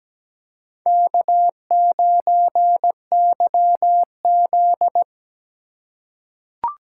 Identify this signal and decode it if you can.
Morse code, heard as a keyed tone, about 17 wpm, 700 Hz. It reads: K9YZ
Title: USA Call Signs - 4 Characters - 2x Character Spacing - 17wpm